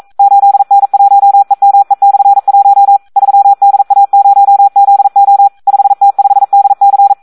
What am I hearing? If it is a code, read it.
Morse code, heard as a keyed tone, about 42 wpm, 800 Hz. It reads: 8N1EME/1 3DA0ZO 5T5DC
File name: Audio practica CW / SD 1 42